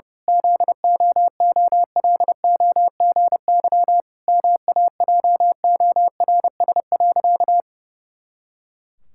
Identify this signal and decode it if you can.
Morse code, heard as a keyed tone, about 32 wpm, 700 Hz. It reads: ZOOLOGY MAJORS.
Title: Pangram Challenge v2 - 30wpm